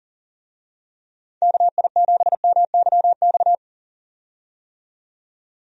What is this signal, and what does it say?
Morse code, heard as a keyed tone, about 40 wpm, 700 Hz. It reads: KI7MYX
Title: USA Call Signs - 6 Characters - 40wpm